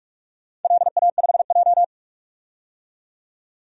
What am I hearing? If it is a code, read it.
Morse code, heard as a keyed tone, about 42 wpm, 700 Hz. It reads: LA5J
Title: DXCC-100 Call Signs - 4 Characters - 45wpm